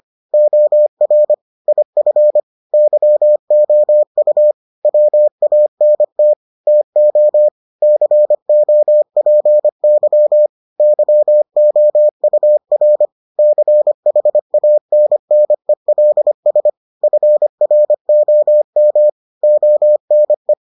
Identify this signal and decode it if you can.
Morse code, heard as a keyed tone, about 25 wpm, 600 Hz. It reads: OR IF YOU WANT TO COPY YOUR CHANNELS FROM ONE